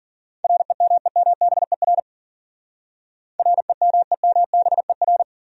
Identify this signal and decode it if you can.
Morse code, heard as a keyed tone, about 52 wpm, 700 Hz. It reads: REMEMBER REMEMBER